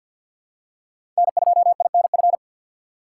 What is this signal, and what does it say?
Morse code, heard as a keyed tone, about 53 wpm, 700 Hz. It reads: N2INF
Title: USA Call Signs - 5 Characters - 8x Character Spacing - 50wpm